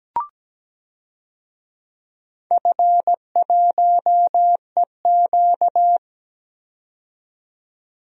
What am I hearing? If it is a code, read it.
Morse code, heard as a keyed tone, about 17 wpm, 700 Hz. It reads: F1EQ
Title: DXCC-100 Call Signs - 4 Characters - 17wpm